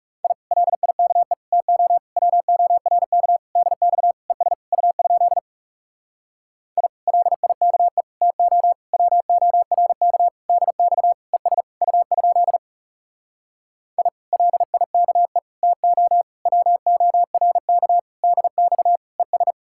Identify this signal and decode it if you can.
Morse code, heard as a keyed tone, about 40 wpm, 700 Hz. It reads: I LIKE TO WORK DX ES U? I LIKE TO WORK DX ES U? I LIKE TO WORK DX ES